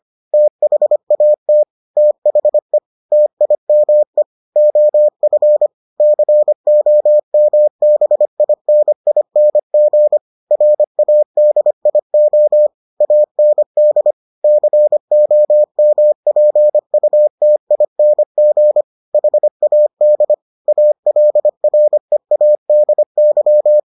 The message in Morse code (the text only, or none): THAT THE TIME OF COMBINING RADIO AND COMPUTING HAD ALREADY